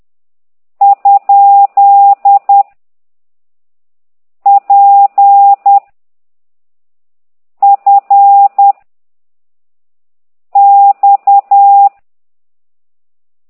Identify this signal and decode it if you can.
Morse code, heard as a keyed tone, about 10 wpm, 800 Hz. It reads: ? P F X
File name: ct2idw / exerc11